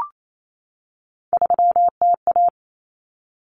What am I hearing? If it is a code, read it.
Morse code, heard as a keyed tone, about 28 wpm, 700 Hz. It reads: 3TA
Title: Common Cut Numbers - 3-Digit 28wpm